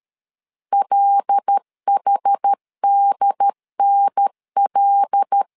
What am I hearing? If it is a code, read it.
Morse code, heard as a keyed tone, about 13 wpm, 800 Hz. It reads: LHDNL
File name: LEZIONI / 5 LBVC / 5 L 12wpm 1913